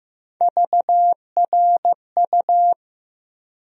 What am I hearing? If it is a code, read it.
Morse code, heard as a keyed tone, about 15 wpm, 700 Hz. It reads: VRU
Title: USA Call Sign Postfixes - 3 Letters 15wpm